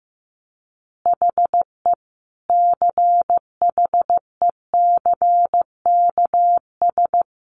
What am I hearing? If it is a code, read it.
Morse code, heard as a keyed tone, about 15 wpm, 700 Hz. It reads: HE CHECKS
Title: Easy Phrases 15wpm